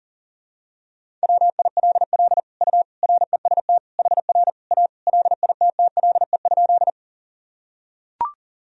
Morse code, heard as a keyed tone, about 40 wpm, 700 Hz. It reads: WILL U REST HR A LITTLE?
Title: Sentences from Top 500 Words and Abbreviations 40wpm